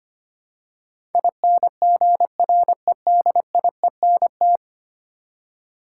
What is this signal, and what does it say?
Morse code, heard as a keyed tone, about 26 wpm, 700 Hz. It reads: INGREDIENT
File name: 10 Letter Words - First 3 Letters Given - 25wpm